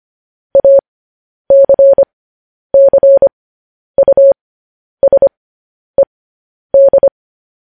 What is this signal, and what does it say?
Morse code, heard as a keyed tone, about 25 wpm, spaced as for 10 wpm, 550 Hz. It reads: ACCUSED
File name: podcast / News headlines-1x-25x10